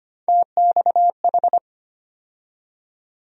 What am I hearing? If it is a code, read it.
Morse code, heard as a keyed tone, about 25 wpm, 700 Hz. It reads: TXH
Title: USA Call Sign Postfixes - 3 Letters 25wpm